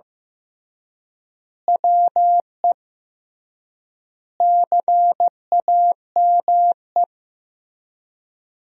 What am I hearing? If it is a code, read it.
Morse code, heard as a keyed tone, about 15 wpm, 700 Hz. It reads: WE CAME
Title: 3x Word-Spacing - Sentences from Top 400 Words - 15wpm